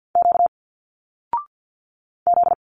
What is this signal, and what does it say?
Morse code, heard as a keyed tone, about 50 wpm, 700 Hz. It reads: Q Z
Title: Single Letters 50wpm